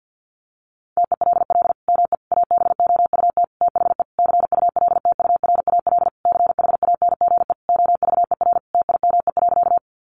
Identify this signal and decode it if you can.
Morse code, heard as a keyed tone, about 50 wpm, 700 Hz. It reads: TELL ME ABOUT THE CULTURAL CHANGE OVER TIME.
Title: Sentences from Top 1500 Words 50wpm